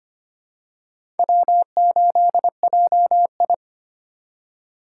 Morse code, heard as a keyed tone, about 25 wpm, 700 Hz. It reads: W8JI
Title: USA Call Signs - 2x Character Spacing - 25wpm